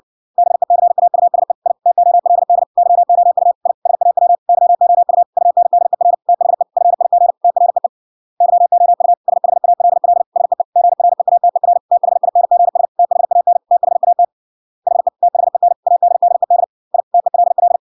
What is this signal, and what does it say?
Morse code, heard as a keyed tone, about 56 wpm, 700 Hz. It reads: BECAUSE I TOLD YOU I SAW YOU UNDER THE FIG TREE YOU SHALL SEE GREATER THINGS THAN THAT HE THEN ADDED I TELL